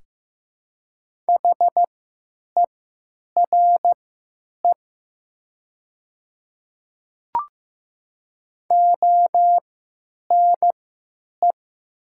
Morse code, heard as a keyed tone, about 15 wpm, spaced as for 9 wpm, 700 Hz. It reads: HERE ONE
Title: 3x Character Spacing - Top 300 Words - 15wpm